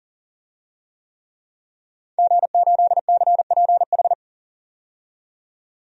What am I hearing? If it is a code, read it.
Morse code, heard as a keyed tone, about 40 wpm, 700 Hz. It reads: G8CPH
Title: DXCC-100 Call Signs - 5 Characters - 40wpm